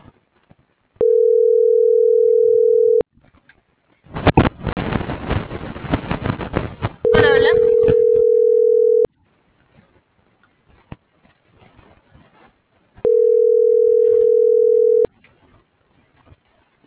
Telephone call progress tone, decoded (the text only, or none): ringback tone